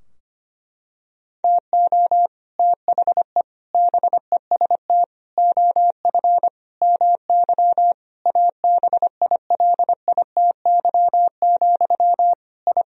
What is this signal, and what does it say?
Morse code, heard as a keyed tone, about 25 wpm, 700 Hz. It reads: TO THE BEST OF MY ABILITY, I
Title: Sentences from Top 1000 Words - Review Part 3 - 25wpm